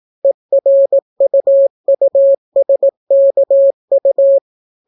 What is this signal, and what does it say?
Morse code, heard as a keyed tone, about 18 wpm, 550 Hz. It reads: ERUUSKU